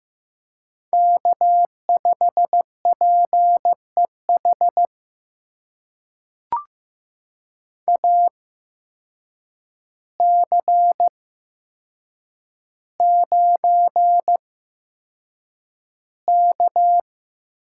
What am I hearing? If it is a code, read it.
Morse code, heard as a keyed tone, about 15 wpm, spaced as for 7 wpm, 700 Hz. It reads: K5PEH AC9K
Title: USA Call Signs - 8x Character Spacing - 15wpm